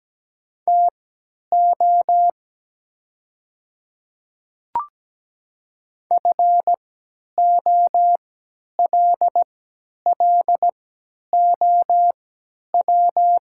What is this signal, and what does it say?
Morse code, heard as a keyed tone, about 17 wpm, spaced as for 10 wpm, 700 Hz. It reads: TO FOLLOW
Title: 3x Character Spacing - Top 300 Words - 17wpm